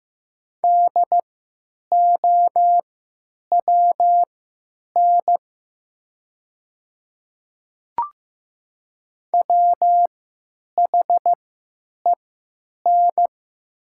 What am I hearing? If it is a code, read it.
Morse code, heard as a keyed tone, about 15 wpm, spaced as for 9 wpm, 700 Hz. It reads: DOWN WHEN
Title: 3x Character Spacing - Top 200 Words - 15wpm